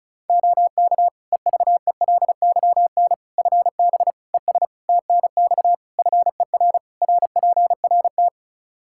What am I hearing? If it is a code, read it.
Morse code, heard as a keyed tone, about 36 wpm, 700 Hz. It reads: OK EVELYN FB ES TNX FER RPRT
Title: Practice QSO - Exchange 2 - Info - 35wpm